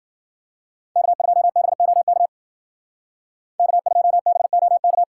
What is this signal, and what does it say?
Morse code, heard as a keyed tone, about 52 wpm, 700 Hz. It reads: K2BOK K2BOK